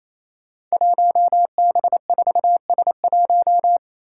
Morse code, heard as a keyed tone, about 28 wpm, 700 Hz. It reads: 1B4S1
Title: Five Challenging Letter-Number Combinations 28wpm